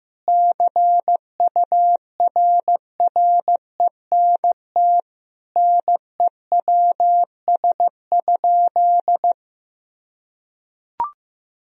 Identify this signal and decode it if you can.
Morse code, heard as a keyed tone, about 16 wpm, 700 Hz. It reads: CURRENT NEWS?
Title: Sentences from Top 1000 Words - Review Part 4 - 15wpm